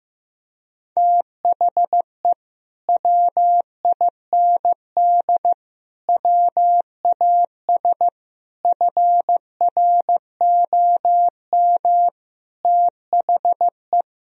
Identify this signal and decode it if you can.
Morse code, heard as a keyed tone, about 15 wpm, 700 Hz. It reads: THE WIND WAS FROM THE
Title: Sentences from Top 1000 Words - Review Part 2 - 15wpm